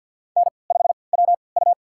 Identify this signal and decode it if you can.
Morse code, heard as a keyed tone, about 52 wpm, spaced as for 29 wpm, 700 Hz. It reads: N5WU
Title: USA Call Signs - 4 Characters - 3x Character Spacing - 50wpm